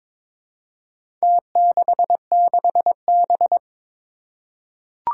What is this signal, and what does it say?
Morse code, heard as a keyed tone, about 22 wpm, 700 Hz. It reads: T66B
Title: Four Challenging Letter-Number Combinations 22wpm